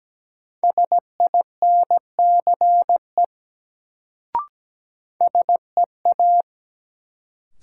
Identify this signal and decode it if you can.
Morse code, heard as a keyed tone, about 17 wpm, 700 Hz. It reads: SINCE SEA
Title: Sentences from Top 300 Words - New Word Drill - 17wpm